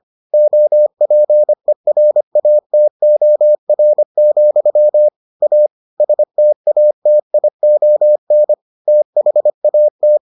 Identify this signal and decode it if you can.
Morse code, heard as a keyed tone, about 25 wpm, 600 Hz. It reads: OPERATOR, A STATION THAT